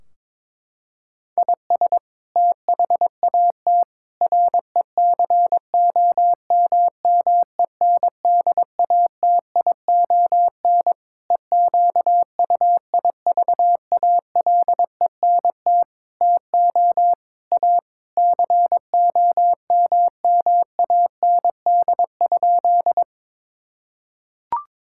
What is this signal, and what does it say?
Morse code, heard as a keyed tone, about 22 wpm, 700 Hz. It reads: IS THAT RECOMMENDATION EQUIVALENT TO A COMMAND?
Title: Sentences from Top 1600 Words 22wpm